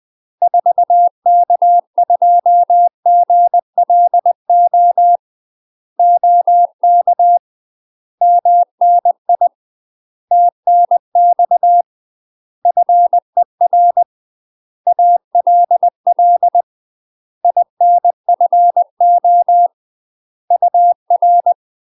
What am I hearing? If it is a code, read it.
Morse code, heard as a keyed tone, about 20 wpm, 700 Hz. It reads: 4K2GLO OK MNI TNX FER ALL INFO UR
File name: UK MASTER QSOs TRAINING / 20WPM / UK MASTER QSOs 0038